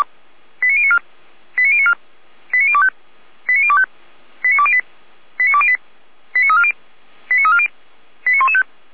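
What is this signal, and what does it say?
Five-tone selective call (ZVEI), heard as quick runs of five tones, tones of about 70 ms. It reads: 89094 89094 89025 89025 89208 89208 89340 89340 89115
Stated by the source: ZVEI five-tone call received by SDR.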